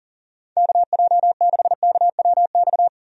Morse code, heard as a keyed tone, about 40 wpm, 700 Hz. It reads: KJ6KWX
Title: USA Call Signs - 6 Characters - 40wpm